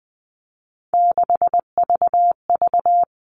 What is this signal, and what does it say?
Morse code, heard as a keyed tone, about 20 wpm, 700 Hz. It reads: 6VV
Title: All Cut Numbers - 3-Digit 20wpm